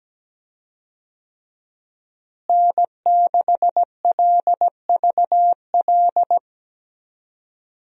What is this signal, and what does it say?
Morse code, heard as a keyed tone, about 17 wpm, 700 Hz. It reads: N6LVL
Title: USA Call Signs - 5 Characters - Repeated 2x - 17wpm